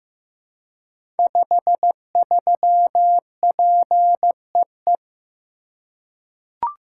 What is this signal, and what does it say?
Morse code, heard as a keyed tone, about 15 wpm, 700 Hz. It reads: 53PEE
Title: Code Groups - 5 Letter:Numbers - 15wpm